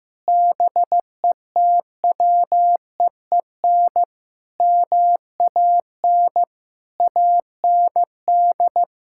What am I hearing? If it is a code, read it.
Morse code, heard as a keyed tone, about 15 wpm, 700 Hz. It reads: BETWEEN MAN AND
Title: Sentences from Top 1600 Words 15wpm